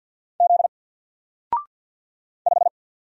Morse code, heard as a keyed tone, about 50 wpm, 700 Hz. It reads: Z 5
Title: Single Letter-Number 50wpm